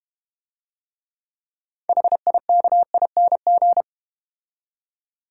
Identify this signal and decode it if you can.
Morse code, heard as a keyed tone, about 32 wpm, 700 Hz. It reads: HIKING